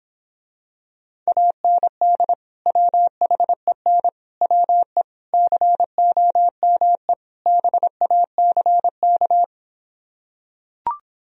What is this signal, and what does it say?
Morse code, heard as a keyed tone, about 26 wpm, 700 Hz. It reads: AND WHEN WE COME BACK